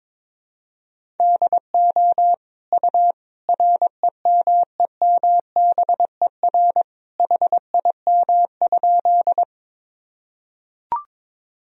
Morse code, heard as a keyed tone, about 22 wpm, 700 Hz. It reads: DO U REMEMBER HIM?
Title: Sentences from Top 400 Words and Abbreviations 22wpm